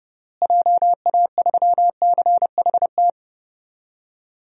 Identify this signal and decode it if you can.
Morse code, heard as a keyed tone, about 30 wpm, 700 Hz. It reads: JA3CHT
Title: DXCC-100 Call Signs - 6 Characters - Repeated 3x - 30wpm